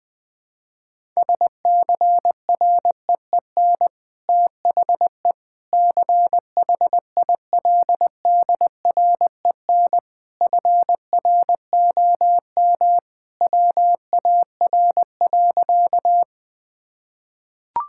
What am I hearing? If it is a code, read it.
Morse code, heard as a keyed tone, about 20 wpm, 700 Hz. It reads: SCREEN THE CHILDREN FROM WAR.
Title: Sentences from Top 1000 Words - Review Part 6 - 20wpm